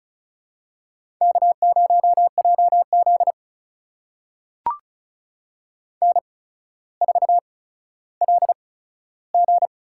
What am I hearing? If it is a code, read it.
Morse code, heard as a keyed tone, about 36 wpm, spaced as for 15 wpm, 700 Hz. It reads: K0JZ N4LG
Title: USA Call Signs - 4 Characters - 8x Character Spacing - 35wpm